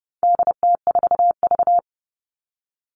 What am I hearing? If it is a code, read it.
Morse code, heard as a keyed tone, about 30 wpm, 700 Hz. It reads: DT4V
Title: All Cut Numbers - 4-Digit 30wpm